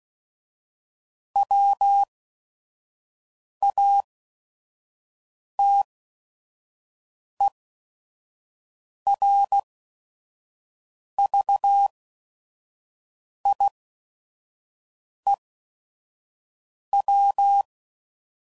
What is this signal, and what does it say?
Morse code, heard as a keyed tone, about 16 wpm, spaced as for 5 wpm, 800 Hz. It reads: WATERVIEW
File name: CW Code Groups @5WPM / fromSM's / Cities